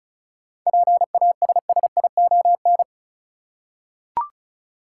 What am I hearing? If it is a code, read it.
Morse code, heard as a keyed tone, about 36 wpm, 700 Hz. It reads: PASSION
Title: Seven Letter Words - 35wpm